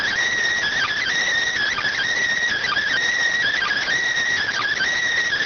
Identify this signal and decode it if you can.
Emergency Alert System (SAME) tones, heard as harsh buzzing bursts. It is heard from 0.7 to 4.9 s.